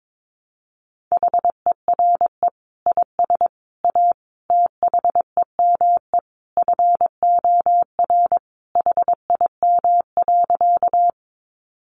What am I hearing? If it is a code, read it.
Morse code, heard as a keyed tone, about 22 wpm, 700 Hz. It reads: HERE IS A THEME FOR HIM.
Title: Sentences from Top 1100 Words 22wpm